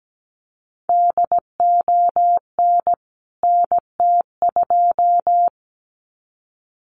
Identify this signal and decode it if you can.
Morse code, heard as a keyed tone, about 17 wpm, 700 Hz. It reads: DON NT2
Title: CWT Exchange 17wpm